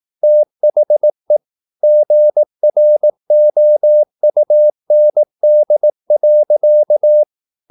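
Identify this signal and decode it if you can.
Morse code, heard as a keyed tone, about 18 wpm, 600 Hz. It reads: THE GROUND.
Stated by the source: Morse practice at 18 wpm, English text, from CW med Gnister.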